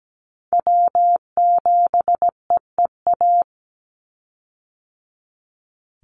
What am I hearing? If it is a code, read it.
Morse code, heard as a keyed tone, about 17 wpm, 700 Hz. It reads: W7EEA
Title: USA Call Signs 17wpm